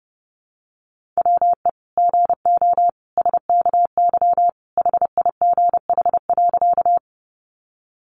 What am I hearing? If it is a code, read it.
Morse code, heard as a keyed tone, about 30 wpm, 700 Hz. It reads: WE GO SKY HIGH.
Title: Sentences from Top 1200 Words 30wpm